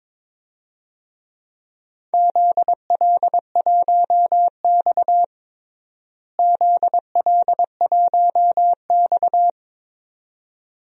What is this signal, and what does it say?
Morse code, heard as a keyed tone, about 22 wpm, 700 Hz. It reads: ZL1X ZL1X
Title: DXCC-100 Call Signs - 4 Characters - Repeated 2x - 22wpm